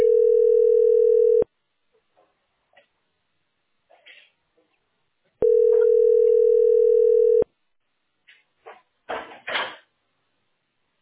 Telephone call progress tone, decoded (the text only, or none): ringback tone